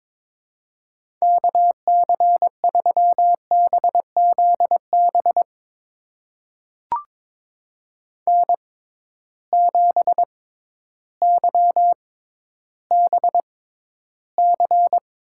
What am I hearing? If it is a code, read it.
Morse code, heard as a keyed tone, about 22 wpm, spaced as for 12 wpm, 700 Hz. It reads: KC3BZB N7YBC
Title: USA Call Signs - 6x Character Spacing - 22wpm